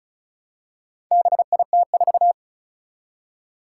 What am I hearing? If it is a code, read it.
Morse code, heard as a keyed tone, about 36 wpm, 700 Hz. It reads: BIT4
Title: Four Challenging Letter-Number Combinations 35wpm